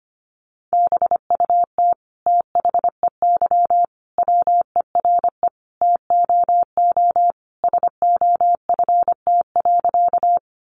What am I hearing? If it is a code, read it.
Morse code, heard as a keyed tone, about 25 wpm, 700 Hz. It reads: BUT THEY WERE TOO SOFT.
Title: Sentences from Top 1500 Words 25wpm